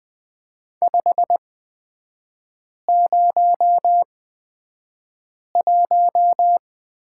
Morse code, heard as a keyed tone, about 20 wpm, spaced as for 5 wpm, 700 Hz. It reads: 501
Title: 243 Lesson 39e - Letter z - Three Characters 20wpm